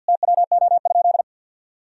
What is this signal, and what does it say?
Morse code, heard as a keyed tone, about 52 wpm, 700 Hz. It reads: TWO?
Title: Sentences from Top 1000 Words - Review Part 1 - 50wpm